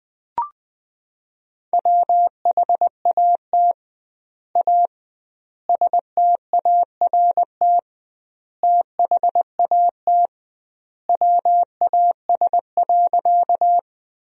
Morse code, heard as a keyed tone, about 20 wpm, 700 Hz. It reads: WHAT A START THAT WAS.
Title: 2x Word-Spacing - Sentences from Top 300 Words - 20wpm